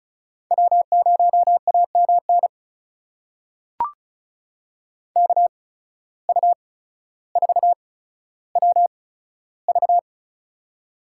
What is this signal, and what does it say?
Morse code, heard as a keyed tone, about 36 wpm, spaced as for 15 wpm, 700 Hz. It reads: W0AMN KU4WV
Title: USA Call Signs - 5 Characters - 8x Character Spacing - 35wpm